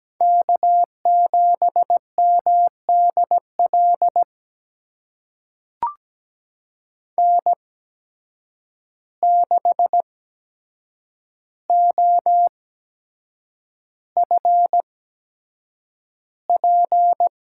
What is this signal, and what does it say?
Morse code, heard as a keyed tone, about 17 wpm, spaced as for 7 wpm, 700 Hz. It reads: K7MDL N6OFP
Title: USA Call Signs - 8x Character Spacing - 17wpm